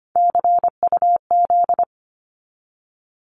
Morse code, heard as a keyed tone, about 25 wpm, 700 Hz. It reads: CUZ
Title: Abbreviations 25wpm